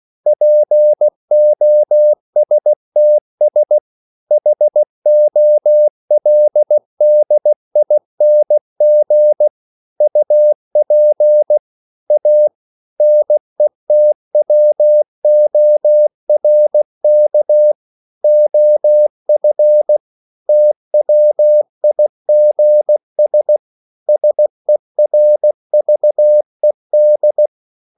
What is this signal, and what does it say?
Morse code, heard as a keyed tone, about 16 wpm, 600 Hz. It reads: POSTS HOLDING UP A NETWORK OF TWIGS SERVED